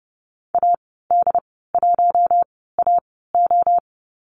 Morse code, heard as a keyed tone, about 30 wpm, spaced as for 17 wpm, 700 Hz. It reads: AD1AO
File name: Lesson 14g - Number 2 - Call Signs 30wpm